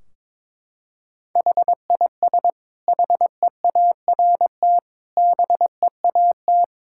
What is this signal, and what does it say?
Morse code, heard as a keyed tone, about 22 wpm, 700 Hz. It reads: HIS HEART BEAT
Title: Sentences from Top 1600 Words 22wpm